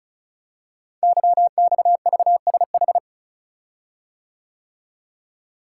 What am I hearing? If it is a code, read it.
Morse code, heard as a keyed tone, about 36 wpm, 700 Hz. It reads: YXVSH
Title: Code Groups - 5 Letter:Numbers - 35wpm